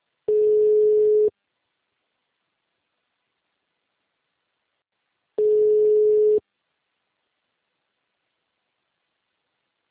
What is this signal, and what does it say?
Ringback tone (Europe, 425 Hz).
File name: out- -2007- - - .9659